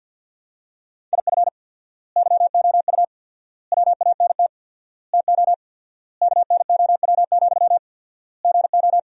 Morse code, heard as a keyed tone, about 52 wpm, 700 Hz. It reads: IF YOU WANT TO KNOW, GO
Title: 4x Word-Spacing - Sentences from Top 200 Words - 50wpm